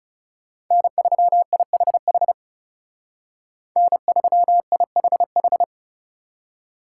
Morse code, heard as a keyed tone, about 35 wpm, 700 Hz. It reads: N3IHH N3IHH
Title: Speed Racing - USA 1x3 Call Signs 30wpm